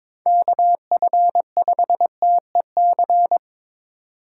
Morse code, heard as a keyed tone, about 22 wpm, 700 Hz. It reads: KF5TEC
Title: USA Call Signs - 6 Characters - 4x Character Spacing - 22wpm